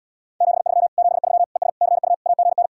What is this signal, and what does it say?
Morse code, heard as a keyed tone, about 80 wpm, 700 Hz. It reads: 73 73 ES CU AGN